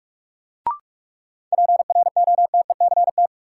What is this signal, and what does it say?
Morse code, heard as a keyed tone, about 46 wpm, 700 Hz. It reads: PROTECT
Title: Sentences from Top 900 Words - New Word Drill - 45wpm